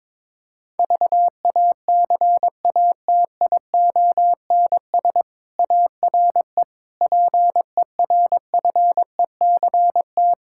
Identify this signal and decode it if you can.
Morse code, heard as a keyed tone, about 21 wpm, 700 Hz. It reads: VACATIONS ARE PERFECT
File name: Difficult Sentences with Hints - 22wpm